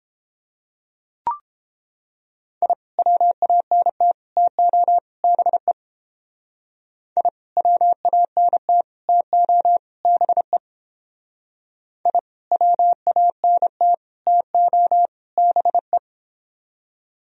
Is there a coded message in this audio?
Yes: Morse code from 2.6 to 16.0 s, heard as a keyed tone, about 31 wpm, 700 Hz. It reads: I WANT TO BE I WANT TO BE I WANT TO BE